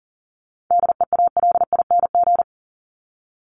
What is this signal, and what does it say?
Morse code, heard as a keyed tone, about 40 wpm, 700 Hz. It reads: DEALING
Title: ING Suffix - Top 100 Words with Base Words 40wpm